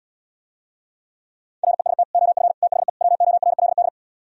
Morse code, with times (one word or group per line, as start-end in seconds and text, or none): 1.6-2.0 REST
2.1-2.5 OF
2.6-2.9 THE
3.0-3.9 WORLD